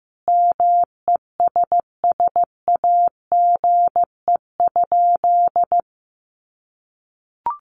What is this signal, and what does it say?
Morse code, heard as a keyed tone, about 15 wpm, 700 Hz. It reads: MESSAGE?
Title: Sentences from Top 1100 Words 15wpm